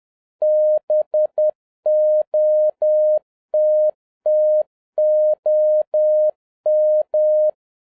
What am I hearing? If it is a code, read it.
Morse code, heard as a keyed tone, about 10 wpm, 610 Hz. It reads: BOTTOM